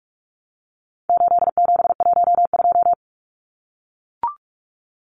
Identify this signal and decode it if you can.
Morse code, heard as a keyed tone, about 44 wpm, 700 Hz. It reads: 8712
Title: Numbers - 4-Digit 45wpm